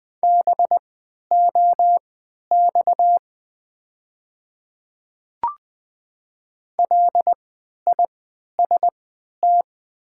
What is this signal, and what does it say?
Morse code, heard as a keyed tone, about 20 wpm, spaced as for 11 wpm, 700 Hz. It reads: BOX LIST
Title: 3x Character Spacing - Top 500 Words - 20wpm